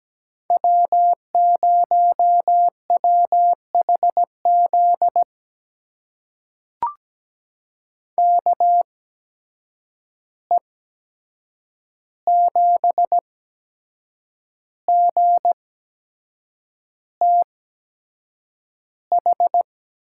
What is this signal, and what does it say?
Morse code, heard as a keyed tone, about 17 wpm, spaced as for 7 wpm, 700 Hz. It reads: W0WHZ KE7GTH
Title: USA Call Signs - 8x Character Spacing - 17wpm